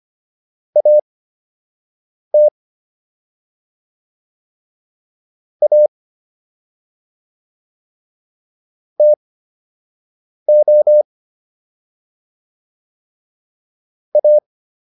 Morse code, heard as a keyed tone, about 25 wpm, spaced as for 6 wpm, 600 Hz. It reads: AT A TO A